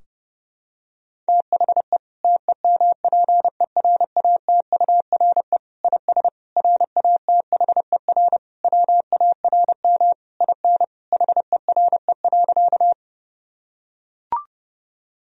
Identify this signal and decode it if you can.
Morse code, heard as a keyed tone, about 30 wpm, 700 Hz. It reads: THE TEMPERATURE IS RATHER WARM IN HERE.
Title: Sentences from Top 1000 Words - Review Part 4 - 30wpm